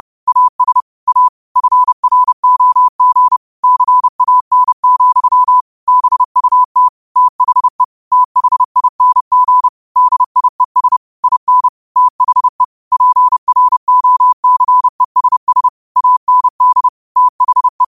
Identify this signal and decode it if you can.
Morse code, heard as a keyed tone, about 30 wpm, 1000 Hz. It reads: AS A FROG CAN, BUT THE THING DIES IN THE PROCESS AND THE
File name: qotdfiles / 30wpm